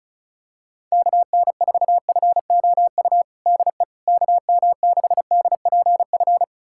Morse code, heard as a keyed tone, about 36 wpm, 700 Hz. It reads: KN4FOU DE KM6DPF
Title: USA Callsign DE Callsign - 6 Characters - 35wpm